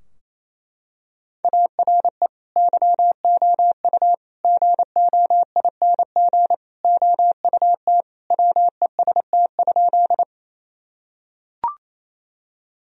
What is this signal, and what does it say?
Morse code, heard as a keyed tone, about 28 wpm, 700 Hz. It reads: ARE YOU GOING OUT WEST?
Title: Sentences from Top 1000 Words - Review Part 2 - 28wpm